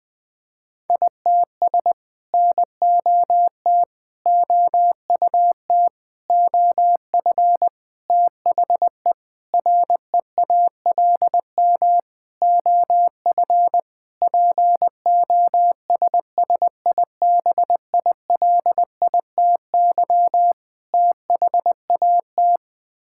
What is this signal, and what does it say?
Morse code, heard as a keyed tone, about 20 wpm, 700 Hz. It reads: ITS NOT OUT OF THE REALM OF POSSIBILITY THAT
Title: Sets of 9 Words - 20wpm